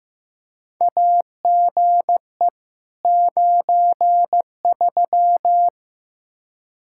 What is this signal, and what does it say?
Morse code, heard as a keyed tone, about 15 wpm, 700 Hz. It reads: AGE 93
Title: Practice QSO - Exchange 3 - More Info - 15wpm